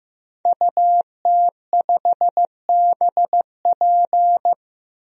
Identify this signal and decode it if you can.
Morse code, heard as a keyed tone, about 15 wpm, 700 Hz. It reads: UT5BP
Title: DXCC-100 Call Signs - 5 Characters - Repeated 3x - 15wpm